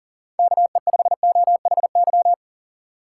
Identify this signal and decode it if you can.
Morse code, heard as a keyed tone, about 40 wpm, 700 Hz. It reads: KE5OHY
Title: USA Call Signs - 6 Characters - 6x Character Spacing - 40wpm